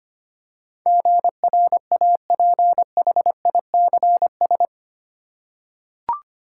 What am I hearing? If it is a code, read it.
Morse code, heard as a keyed tone, about 25 wpm, 700 Hz. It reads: GRAPHICS
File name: Eight Letter Words - 25wpm